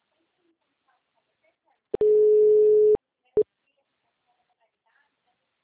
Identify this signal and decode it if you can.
Ringback tone (Europe, 425 Hz), with speech.
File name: out- -2051- - - .12305